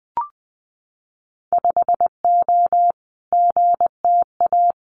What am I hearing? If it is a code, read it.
Morse code, heard as a keyed tone, about 20 wpm, 700 Hz. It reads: 5O GTA
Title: Winter Field Day Exchanges Encore 20wpm